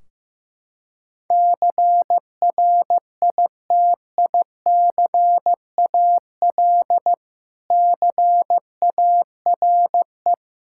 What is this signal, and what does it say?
Morse code, heard as a keyed tone, about 15 wpm, 700 Hz. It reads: CRITICAL CARE
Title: Practice QSO - Exchange 3 - More Info - 15wpm